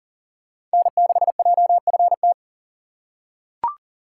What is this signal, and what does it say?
Morse code, heard as a keyed tone, about 38 wpm, 700 Hz. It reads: N6JFT
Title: USA Call Signs - 4x Character Spacing - 40wpm